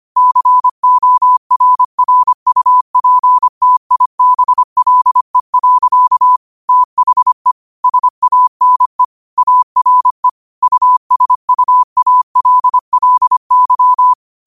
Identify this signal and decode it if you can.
Morse code, heard as a keyed tone, about 25 wpm, 1000 Hz. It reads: CORRUPTIBLE. THE SANE ARE USUALLY
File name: 25wpm